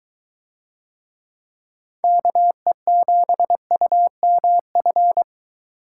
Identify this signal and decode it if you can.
Morse code, heard as a keyed tone, about 22 wpm, 700 Hz. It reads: KE7UMF